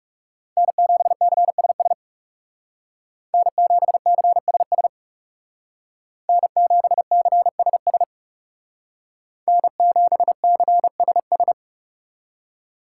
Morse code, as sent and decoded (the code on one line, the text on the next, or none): -. --... -.-. ... ... / -. --... -.-. ... ... / -. --... -.-. ... ... / -. --... -.-. ... ...
N7CSS N7CSS N7CSS N7CSS